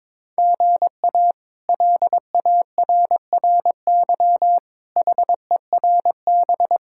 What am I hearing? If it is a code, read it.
Morse code, heard as a keyed tone, about 23 wpm, 700 Hz. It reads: GA LARRY HERB
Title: K1USN SST QSO Practice - 22wpm